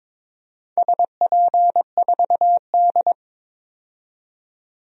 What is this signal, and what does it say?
Morse code, heard as a keyed tone, about 23 wpm, 690 Hz. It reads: SP4D
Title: DXCC-100 Call Signs - 4 Characters - 22wpm